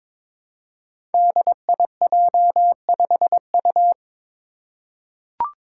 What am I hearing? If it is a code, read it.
Morse code, heard as a keyed tone, about 22 wpm, 700 Hz. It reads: DIJ5U
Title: Five Challenging Letter-Number Combinations 22wpm